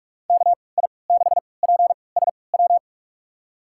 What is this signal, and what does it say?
Morse code, heard as a keyed tone, about 46 wpm, spaced as for 26 wpm, 700 Hz. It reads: KI6PSW
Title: USA Call Signs - 6 Characters - 3x Character Spacing - 45wpm